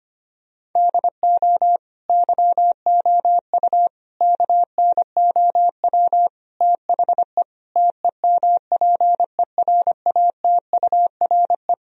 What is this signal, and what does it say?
Morse code, heard as a keyed tone, about 25 wpm, 700 Hz. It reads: DO YOU KNOW THE TEMPERATURE
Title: Sentences from Top 1000 Words - Review Part 4 - 25wpm